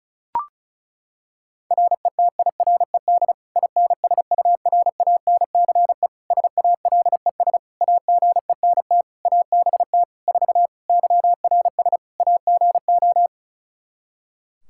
Morse code, heard as a keyed tone, about 34 wpm, 700 Hz. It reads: RETIRED INSURANCE SALES AGENT ABT 4 YRS AGO
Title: K1USN SST QSO Practice 35wpm